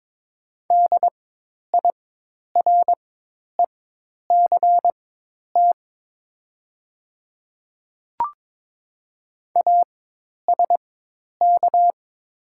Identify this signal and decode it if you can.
Morse code, heard as a keyed tone, about 22 wpm, spaced as for 10 wpm, 700 Hz. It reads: DIRECT ASK
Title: 4x Character Spacing - Top 500 Words - 22wpm